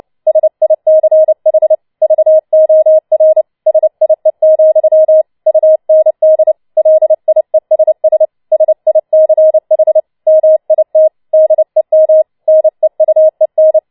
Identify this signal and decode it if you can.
Morse code, heard as a keyed tone, about 29 wpm, 630 Hz. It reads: SICH VOR SIE, UND LIESS SICH MIT DEM NEUEN